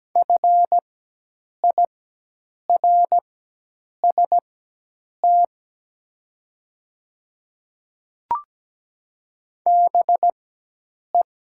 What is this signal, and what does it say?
Morse code, heard as a keyed tone, about 17 wpm, spaced as for 8 wpm, 700 Hz. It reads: FIRST BE
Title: 4x Character Spacing - Top 100 Words - 17wpm